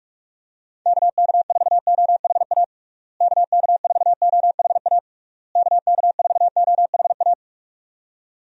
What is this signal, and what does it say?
Morse code, heard as a keyed tone, about 46 wpm, 700 Hz. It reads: KK4OHA KK4OHA KK4OHA
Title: USA Call Signs - 6 Characters - Repeated 3x - 45wpm